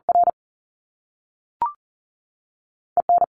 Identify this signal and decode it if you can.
Morse code, heard as a keyed tone, about 40 wpm, spaced as for 3 wpm, 700 Hz. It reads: RE N
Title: Top Two Letter Combinations in Words 40wpm